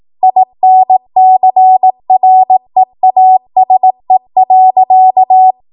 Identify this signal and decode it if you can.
Morse code, heard as a keyed tone, about 18 wpm, 750 Hz. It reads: INCREASE.
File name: Archive / 18 WPM / 18WPM